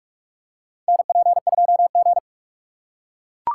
Morse code, heard as a keyed tone, about 44 wpm, 690 Hz. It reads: NP2G
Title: USA Call Signs - 4 Characters - 8x Character Spacing - 45wpm